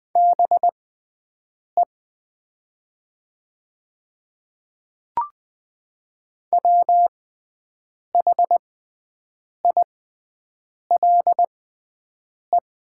Morse code, heard as a keyed tone, about 20 wpm, spaced as for 7 wpm, 700 Hz. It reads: BE WHILE
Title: 6x Character Spacing - Top 300 Words - 20wpm